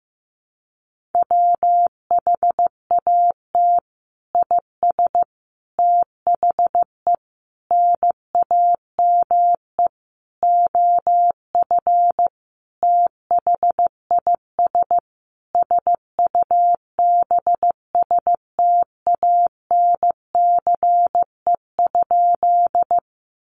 Morse code, heard as a keyed tone, about 15 wpm, 700 Hz. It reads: WHAT IS THE NAME OF THIS SUBSTANCE?
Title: Sentences from Top 1500 Words 15wpm